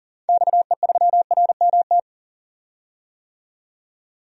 Morse code, heard as a keyed tone, about 40 wpm, 700 Hz. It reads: XE3RMT
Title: DXCC-100 Call Signs - 6 Characters - 40wpm